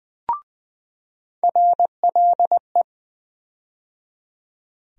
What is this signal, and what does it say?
Morse code, heard as a keyed tone, about 20 wpm, 700 Hz. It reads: RLE
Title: USA Call Sign Postfixes - 3 Letters 20wpm